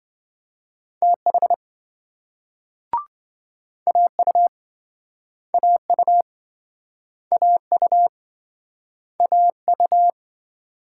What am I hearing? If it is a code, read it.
Morse code, heard as a keyed tone, about 27 wpm, 700 Hz. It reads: TH AU AU AU AU